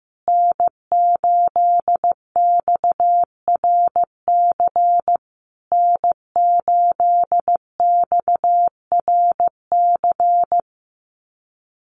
Morse code, heard as a keyed tone, about 15 wpm, 700 Hz. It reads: N8XRC N8XRC
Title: USA Call Signs Repeated 3x 15wpm